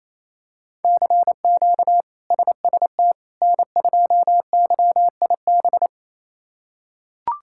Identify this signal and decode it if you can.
Morse code, heard as a keyed tone, about 28 wpm, 700 Hz. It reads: CQ SST N2YIB
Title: K1USN SST QSO Practice - 28wpm